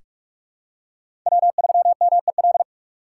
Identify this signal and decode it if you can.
Morse code, heard as a keyed tone, about 42 wpm, 700 Hz. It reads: W3MEL
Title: K1USN SST QSO Practice - 45wpm